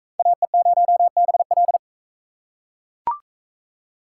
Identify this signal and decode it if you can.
Morse code, heard as a keyed tone, about 39 wpm, 700 Hz. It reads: AE0BL